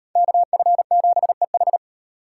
Morse code, heard as a keyed tone, about 34 wpm, 700 Hz. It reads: KF7EH